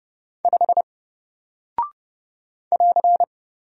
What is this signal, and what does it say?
Morse code, heard as a keyed tone, about 30 wpm, 700 Hz. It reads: 5 <AR>
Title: Single Character (Letter:Number:Punctuation:Prosign) Alt 30wpm